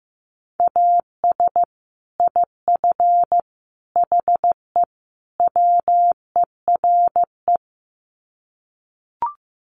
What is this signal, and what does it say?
Morse code, heard as a keyed tone, about 15 wpm, 700 Hz. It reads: AS IF HE WERE
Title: Sets of 4 Words 15wpm